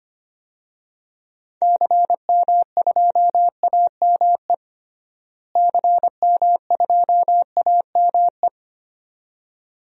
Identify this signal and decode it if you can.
Morse code, heard as a keyed tone, about 25 wpm, 700 Hz. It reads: CM2AME CM2AME
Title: DXCC-100 Call Signs - 6 Characters - Repeated 2x - 25wpm